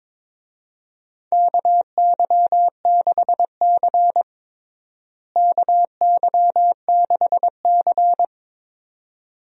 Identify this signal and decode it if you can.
Morse code, heard as a keyed tone, about 22 wpm, 700 Hz. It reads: KY6C KY6C
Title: USA Call Signs - 4 Characters - Repeated 2x - 22wpm